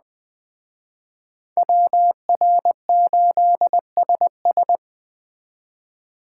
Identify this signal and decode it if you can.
Morse code, heard as a keyed tone, about 21 wpm, 700 Hz. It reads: WR8SS
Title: Speed Racing - USA 2x2 Call Signs 17wpm